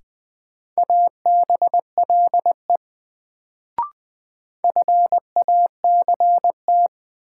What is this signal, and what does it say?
Morse code, heard as a keyed tone, about 20 wpm, 700 Hz. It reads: ABLE FACT
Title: Sets of 2 Words - New Word Drill - 20wpm